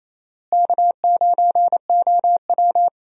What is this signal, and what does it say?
Morse code, heard as a keyed tone, about 28 wpm, 700 Hz. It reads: K9OW
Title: USA Call Signs - 4 Characters - Repeated 3x - 28wpm